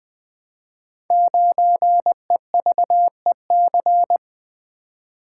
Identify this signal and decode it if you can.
Morse code, heard as a keyed tone, about 20 wpm, 700 Hz. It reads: 9EVEC
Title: Code Groups - 5 Letter:Numbers - 20wpm